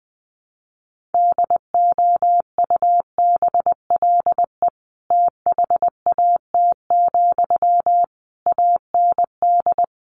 Morse code, heard as a keyed tone, about 20 wpm, 700 Hz. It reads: DOUBLE THAT, AND
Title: Sentences from Top 1400 Words 20wpm